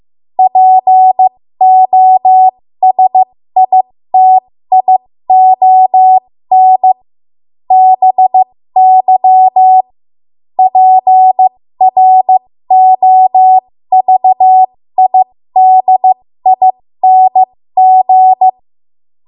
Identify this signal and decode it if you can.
Morse code, heard as a keyed tone, about 15 wpm, spaced as for 13 wpm, 750 Hz. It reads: POSITION BY PROVIDING
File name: Morse / 13WPM